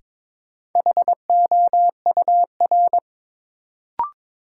Morse code, heard as a keyed tone, about 22 wpm, 700 Hz. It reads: HOUR
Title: Sentences from Top 400 Words - New Word Drill - 22wpm